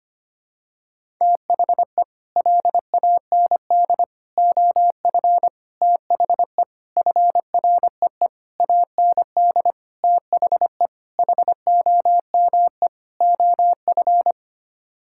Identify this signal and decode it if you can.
Morse code, heard as a keyed tone, about 25 wpm, 700 Hz. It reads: THE LAND OF THE FREE AND THE HOME OF
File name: Sets of 9 Words - 25wpm